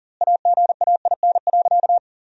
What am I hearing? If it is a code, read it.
Morse code, heard as a keyed tone, about 40 wpm, 700 Hz. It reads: AGAIN.
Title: 2x Word-Spacing - Sentences from Top 300 Words - 40wpm